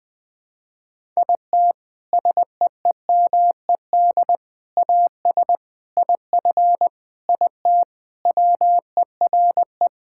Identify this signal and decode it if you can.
Morse code, heard as a keyed tone, about 20 wpm, 700 Hz. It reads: IT SEEMED AS IF IT WERE
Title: Sentences with 4-Word Phrases - Spoken Beginning - 20wpm